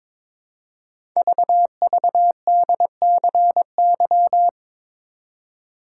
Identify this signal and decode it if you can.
Morse code, heard as a keyed tone, about 22 wpm, 700 Hz. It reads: VVDCY
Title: Code Groups - 5 Letters - 22wpm